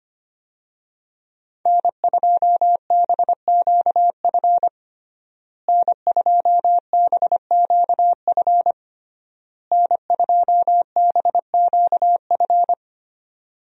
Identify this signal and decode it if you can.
Morse code, heard as a keyed tone, about 25 wpm, 700 Hz. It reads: N2BQF N2BQF N2BQF
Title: USA Call Signs - 5 Characters - Repeated 3x - 25wpm